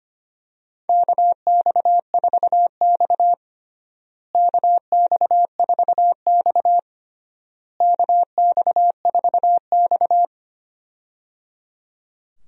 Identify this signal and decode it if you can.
Morse code, heard as a keyed tone, about 25 wpm, 700 Hz. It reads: KX4X KX4X KX4X
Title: USA Call Signs - 4 Characters - Repeated 3x - 25wpm